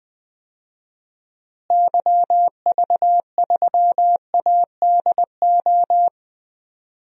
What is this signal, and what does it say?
Morse code, heard as a keyed tone, about 19 wpm, 700 Hz. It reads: YV3ADO
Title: DXCC-100 Call Signs - 6 Characters - Repeated 3x - 20wpm